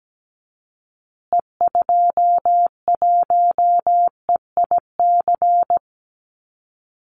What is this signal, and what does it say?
Morse code, heard as a keyed tone, about 17 wpm, 700 Hz. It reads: E21EIC
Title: Call Signs Repeated 3x - 17wpm